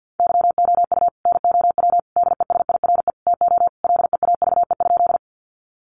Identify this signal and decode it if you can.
Morse code, heard as a keyed tone, about 50 wpm, 700 Hz. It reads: YOU NOW DESIRE TO LEAVE?
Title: Sentences from Top 1100 Words 50wpm